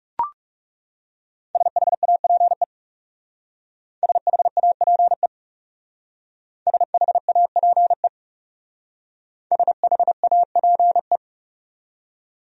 Morse code, heard as a keyed tone, about 37 wpm, 700 Hz. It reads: SHAPE SHAPE SHAPE SHAPE
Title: Speed-Racing Top 500 Words 30wpm